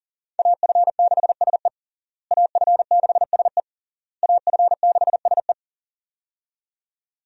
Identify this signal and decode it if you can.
Morse code, heard as a keyed tone, about 40 wpm, 700 Hz. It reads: AF6SE AF6SE AF6SE
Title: USA Call Signs - 5 Characters - Repeated 3x - 40wpm